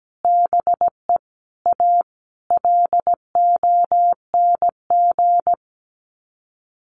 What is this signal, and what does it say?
Morse code, heard as a keyed tone, about 17 wpm, 700 Hz. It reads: BE A LONG
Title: Sets of 5 Words Encore2 17wpm